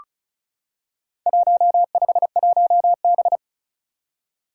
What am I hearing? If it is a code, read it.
Morse code, heard as a keyed tone, about 34 wpm, 700 Hz. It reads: 151B
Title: Four Challenging Letter-Number Combinations 35wpm